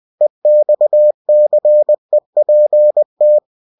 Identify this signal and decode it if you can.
Morse code, heard as a keyed tone, about 20 wpm, 600 Hz. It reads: EXCEPT
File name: cw-books / winnie-the-pooh / winnie-20wpm-page-0105